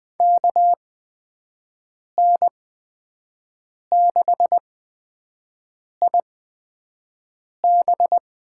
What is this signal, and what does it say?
Morse code, heard as a keyed tone, about 20 wpm, spaced as for 5 wpm, 700 Hz. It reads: KN6IB
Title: USA Call Signs - 5 Characters - 8x Character Spacing - 20wpm